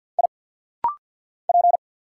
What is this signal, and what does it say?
Morse code, heard as a keyed tone, about 49 wpm, 700 Hz. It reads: I P
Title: Single Letter-Number Alt - Rapid-Fire 50wpm